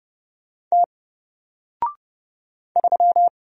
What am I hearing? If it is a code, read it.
Morse code, heard as a keyed tone, about 29 wpm, 700 Hz. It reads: T 3
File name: Single-Letter-Number-30wpm